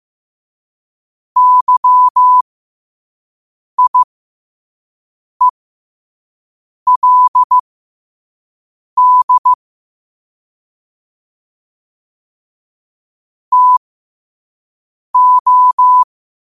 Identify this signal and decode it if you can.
Morse code, heard as a keyed tone, about 15 wpm, spaced as for 5 wpm, 1000 Hz. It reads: YIELD TO
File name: qotdfiles / 5wpm